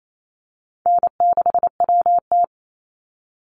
Morse code, heard as a keyed tone, about 28 wpm, 700 Hz. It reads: N6WT
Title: NA SOTA Chaser Callsigns 28wpm